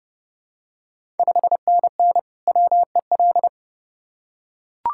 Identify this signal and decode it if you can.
Morse code, heard as a keyed tone, about 31 wpm, 700 Hz. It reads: 5NN WEL
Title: Ontario QSO Party Exchange 30wpm